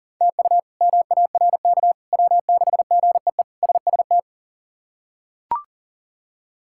Morse code, heard as a keyed tone, about 40 wpm, 700 Hz. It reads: TU MARK W6GEE SST
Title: K1USN SST QSO Practice - 40wpm